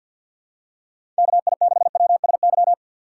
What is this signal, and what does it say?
Morse code, heard as a keyed tone, about 52 wpm, 700 Hz. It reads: KI6WSY
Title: USA Call Signs - 6 Characters - 4x Character Spacing - 50wpm